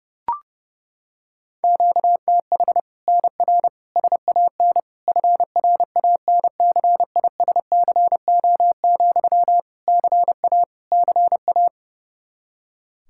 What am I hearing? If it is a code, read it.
Morse code, heard as a keyed tone, about 32 wpm, 700 Hz. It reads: QTH NR SAN FRANCISCO, CA CA
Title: Practice QSO - Exchange 1 - Beginning - 30wpm